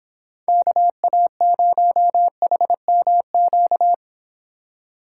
Morse code, heard as a keyed tone, about 25 wpm, 700 Hz. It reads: KA0HMQ